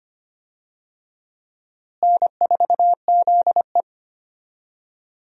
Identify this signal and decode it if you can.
Morse code, heard as a keyed tone, about 25 wpm, 700 Hz. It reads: N4ZE
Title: USA Call Signs - 4 Characters - Repeated 3x - 25wpm